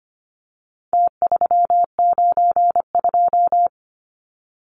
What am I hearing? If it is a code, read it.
Morse code, heard as a keyed tone, about 25 wpm, 700 Hz. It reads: T392
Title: Numbers - Contest Style - 4-Digit 25wpm